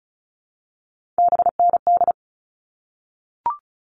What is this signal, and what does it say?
Morse code, heard as a keyed tone, about 34 wpm, 700 Hz. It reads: BND
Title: All Cut Numbers - 3-Digit 35wpm